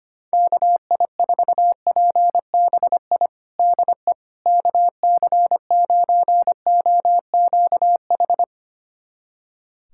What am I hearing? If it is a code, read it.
Morse code, heard as a keyed tone, about 25 wpm, 700 Hz. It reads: KI4PBI DE KC9OQH
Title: USA Callsign DE Callsign 25wpm